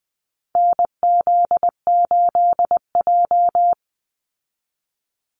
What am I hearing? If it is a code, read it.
Morse code, heard as a keyed tone, about 20 wpm, 700 Hz. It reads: NZ8J
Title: Call Signs Repeated 2x - 20wpm